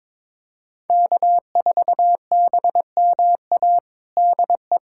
Morse code, heard as a keyed tone, about 22 wpm, 700 Hz. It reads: K4BMA DE
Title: USA Callsign DE Callsign - 5 Characters - 22wpm